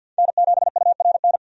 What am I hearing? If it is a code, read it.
Morse code, heard as a keyed tone, about 50 wpm, 700 Hz. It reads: N7URN
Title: USA Call Signs - 5 Characters - 50wpm